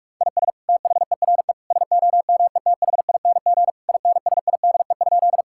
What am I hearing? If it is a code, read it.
Morse code, heard as a keyed tone, about 42 wpm, 700 Hz. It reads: IS THERE SOMETHING INSIDE?
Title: Sentences from Top 1000 Words - Review Part 6 - 45wpm